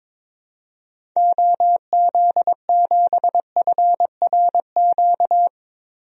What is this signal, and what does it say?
Morse code, heard as a keyed tone, about 22 wpm, 700 Hz. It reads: OZ7FRQ
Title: DXCC-100 Call Signs - 6 Characters - Repeated 3x - 22wpm